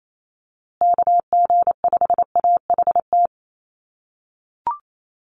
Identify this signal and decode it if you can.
Morse code, heard as a keyed tone, about 28 wpm, 700 Hz. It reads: KG5AHT
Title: USA Call Signs Repeated 3x 28wpm